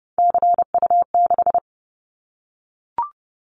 Morse code, heard as a keyed tone, about 30 wpm, 700 Hz. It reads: CU6
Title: Callsign Prefixes for Top 50 DXCC - 30wpm